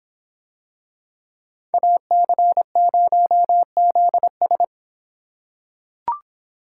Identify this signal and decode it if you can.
Morse code, heard as a keyed tone, about 25 wpm, 700 Hz. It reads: AC0ZS